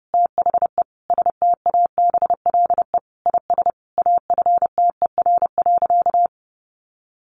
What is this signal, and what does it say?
Morse code, heard as a keyed tone, about 30 wpm, 700 Hz. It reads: THE STABLE IS AFTER.
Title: Sentences from Top 1500 Words 30wpm